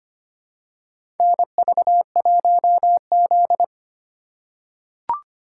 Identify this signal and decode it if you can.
Morse code, heard as a keyed tone, about 25 wpm, 700 Hz. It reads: NV1Z
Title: USA Call Signs - 4 Characters - 2x Character Spacing - 25wpm